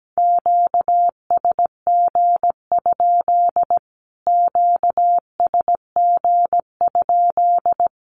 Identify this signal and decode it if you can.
Morse code, heard as a keyed tone, about 17 wpm, 700 Hz. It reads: QSG? QSG?
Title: Q-Codes Repeated 3x 17wpm